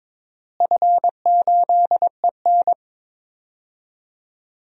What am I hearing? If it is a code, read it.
Morse code, heard as a keyed tone, about 22 wpm, 700 Hz. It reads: F8EN
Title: DXCC-100 Call Signs - 4 Characters - Repeated 2x - 22wpm